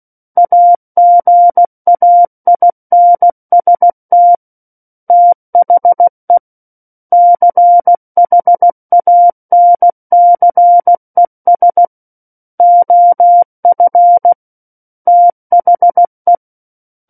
Morse code, heard as a keyed tone, about 16 wpm, 700 Hz. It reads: AGAINST THE CHANCES OF THE
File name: Book One / 07-Chapter 7 - 16 WPM